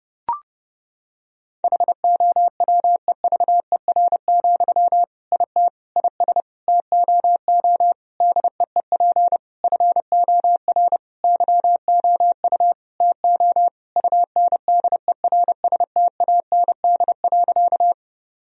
Morse code, heard as a keyed tone, about 30 wpm, 700 Hz. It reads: HOWEVER, IT IS TOO DEEP FOR YOU TO UNDERSTAND.
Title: Sentences from Top 1000 Words - Review Part 3 - 30wpm